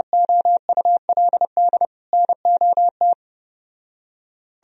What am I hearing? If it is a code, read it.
Morse code, heard as a keyed tone, about 30 wpm, 700 Hz. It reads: OULD NOT